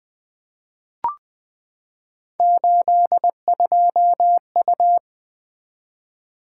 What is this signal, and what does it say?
Morse code, heard as a keyed tone, about 20 wpm, 700 Hz. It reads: 82U
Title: Three Letter-Number Combinations - 20wpm